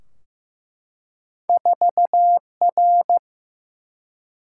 Morse code, heard as a keyed tone, about 15 wpm, 700 Hz. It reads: 4R
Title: Single-Letter Single-Number 15wpm